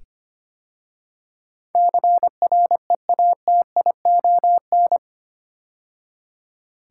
Morse code, heard as a keyed tone, about 25 wpm, 700 Hz. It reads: CREATION
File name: 8 Letter Words - First 3 Letters Given - 25wpm